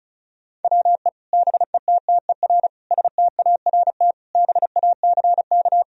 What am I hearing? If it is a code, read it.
Morse code, heard as a keyed tone, about 36 wpm, 700 Hz. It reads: WE BETTER START BACK
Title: Sentences from Top 1000 Words - Review Part 1 - 35wpm